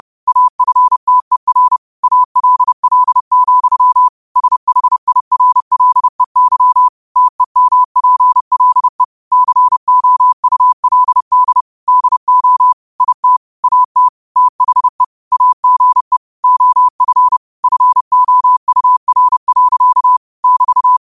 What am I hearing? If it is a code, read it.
Morse code, heard as a keyed tone, about 30 wpm, 1000 Hz. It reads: AFTER ALL, SHIRLEY TEMPLE COULD DO IT AT THE AGE OF FOUR. <BT>